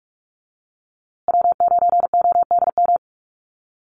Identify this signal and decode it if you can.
Morse code, heard as a keyed tone, about 44 wpm, 700 Hz. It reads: W9ODM
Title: USA Call Signs Repeated 3x 45wpm